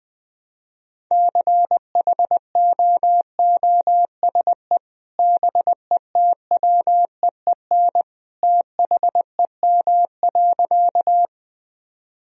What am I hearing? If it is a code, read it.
Morse code, heard as a keyed tone, about 20 wpm, 700 Hz. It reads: CHOOSE BETWEEN THEM.
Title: Sentences from Top 1000 Words - Review Part 4 - 20wpm